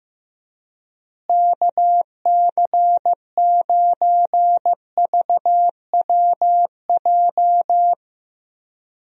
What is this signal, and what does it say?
Morse code, heard as a keyed tone, about 15 wpm, 700 Hz. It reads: KC9VWJ
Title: USA Call Signs - 6 Characters - Repeated 3x - 15wpm